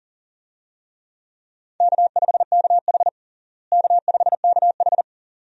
Morse code, heard as a keyed tone, about 40 wpm, 700 Hz. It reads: K5KH K5KH
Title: USA Call Signs - 4 Characters - Repeated 2x - 40wpm